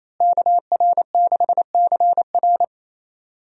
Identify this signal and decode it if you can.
Morse code, heard as a keyed tone, about 28 wpm, 700 Hz. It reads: KR6CR
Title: USA Call Signs - 5 Characters - Repeated 3x - 28wpm